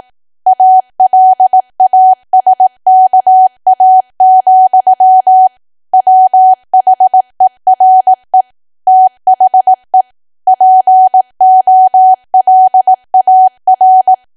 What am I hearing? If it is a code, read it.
Morse code, heard as a keyed tone, about 18 wpm, 750 Hz. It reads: ALASKA, WHERE THE POLAR
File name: cwa / W1AW-Level2 / 18WPM